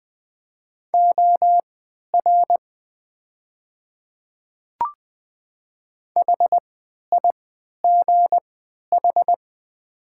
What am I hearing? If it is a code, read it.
Morse code, heard as a keyed tone, about 20 wpm, spaced as for 11 wpm, 700 Hz. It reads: OR HIGH
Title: 3x Character Spacing - Top 200 Words - 20wpm